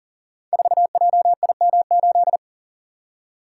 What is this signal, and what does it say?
Morse code, heard as a keyed tone, about 43 wpm, 700 Hz. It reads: 4JIM8
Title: Code Groups - 5 Letter:Numbers - 40wpm